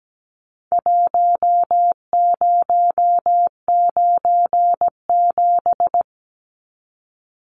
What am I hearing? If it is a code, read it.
Morse code, heard as a keyed tone, about 17 wpm, 700 Hz. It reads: 1097
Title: Numbers - Contest Style - 4-Digit 17wpm